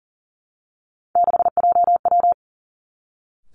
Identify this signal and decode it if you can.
Morse code, heard as a keyed tone, about 39 wpm, 700 Hz. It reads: 6JW
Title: Three Challenging Letter-Number Combinations 40wpm